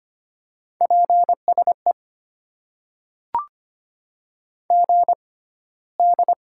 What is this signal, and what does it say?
Morse code, heard as a keyed tone, about 25 wpm, spaced as for 15 wpm, 700 Hz. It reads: PSE GD
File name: Top 90 QSO Elements - 6x Character Spacing - 25wpm